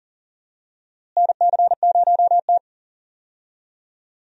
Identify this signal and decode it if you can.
Morse code, heard as a keyed tone, about 40 wpm, 700 Hz. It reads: NC0T
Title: USA Call Signs - 4 Characters - Repeated 3x - 40wpm